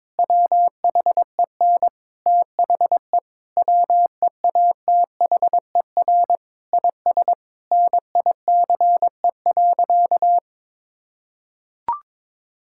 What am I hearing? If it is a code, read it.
Morse code, heard as a keyed tone, about 22 wpm, 700 Hz. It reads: WHEN THE WEATHER IS NICE.
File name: Difficult Sentences with Hints - 22wpm